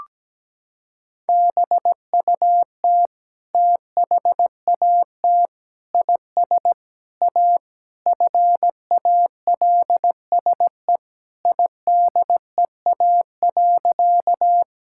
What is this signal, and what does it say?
Morse code, heard as a keyed tone, about 17 wpm, 700 Hz. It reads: BUT THAT IS A FALSE IDEA.
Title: Sentences from Top 1600 Words 17wpm